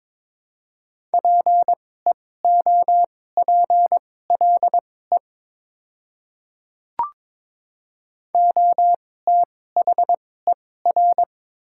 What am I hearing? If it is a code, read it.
Morse code, heard as a keyed tone, about 21 wpm, spaced as for 16 wpm, 700 Hz. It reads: PEOPLE OTHER
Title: 2x Character Spacing - Top 200 Words - 22wpm